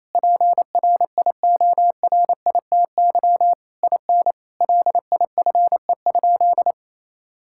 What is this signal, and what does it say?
Morse code, heard as a keyed tone, about 28 wpm, 700 Hz. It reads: PRIORITY IN LIFE?
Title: Sentences from Top 1600 Words 28wpm